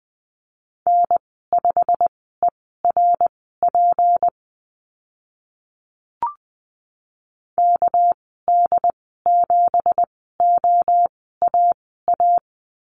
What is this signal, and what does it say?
Morse code, heard as a keyed tone, about 20 wpm, spaced as for 14 wpm, 700 Hz. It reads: N5ERP KD7OAA
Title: 219 Lesson 35g - Number 0 - Call Signs 20wpm